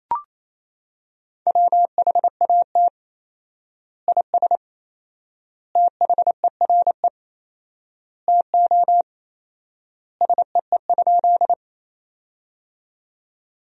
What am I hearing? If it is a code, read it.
Morse code, heard as a keyed tone, about 28 wpm, 700 Hz. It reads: WHAT IS THERE TO SEE?
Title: 4x Word-Spacing - Sentences from Top 100 Words - 28wpm